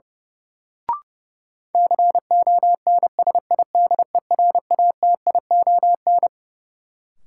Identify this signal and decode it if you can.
Morse code, heard as a keyed tone, about 32 wpm, 700 Hz. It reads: CONSIDERATION
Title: Sentences from Top 1500 Words - New Word Drill - 30wpm